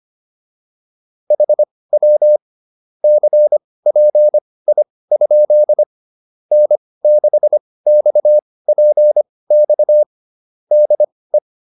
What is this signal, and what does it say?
Morse code, heard as a keyed tone, about 25 wpm, spaced as for 18 wpm, 600 Hz. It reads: HW CPI? N6XPX DE
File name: QSO306 18